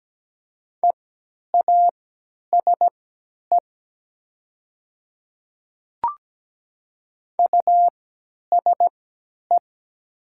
Morse code, heard as a keyed tone, about 17 wpm, spaced as for 10 wpm, 700 Hz. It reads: EASE USE
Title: 3x Character Spacing - Top 300 Words - 17wpm